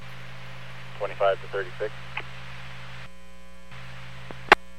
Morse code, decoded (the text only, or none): none